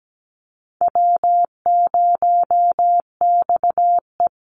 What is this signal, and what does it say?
Morse code, heard as a keyed tone, about 17 wpm, 700 Hz. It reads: W0XE
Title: Easy Call Signs Repeated 2x - 17wpm